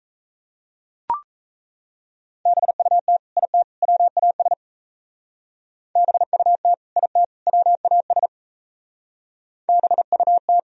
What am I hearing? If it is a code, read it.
Morse code, heard as a keyed tone, about 34 wpm, 700 Hz. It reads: BUT IT WAS BUT IT WAS BUT
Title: Speed-Racing - Sets of 3 Words 28wpm